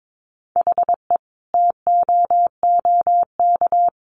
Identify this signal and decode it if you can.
Morse code, heard as a keyed tone, about 22 wpm, 700 Hz. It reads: HE TOOK
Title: Sentences from Top 600 Words 22wpm